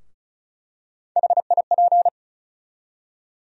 Morse code, heard as a keyed tone, about 36 wpm, 700 Hz. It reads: HIP
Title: USA Call Sign Postfixes - 3 Letters 35wpm